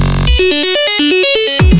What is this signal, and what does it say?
There is no code.